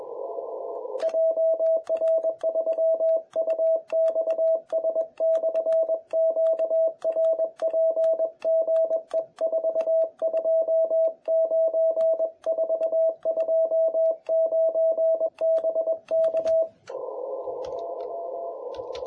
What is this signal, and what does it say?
Morse code, heard as a keyed tone, about 21 wpm, 650 Hz. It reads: JR3UXS/QRP GE 429 429 BK
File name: jh8cbh / 2021cw